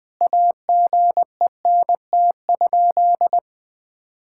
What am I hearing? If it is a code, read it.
Morse code, heard as a keyed tone, about 20 wpm, 700 Hz. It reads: AGENT?
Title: Sentences from Top 1600 Words 20wpm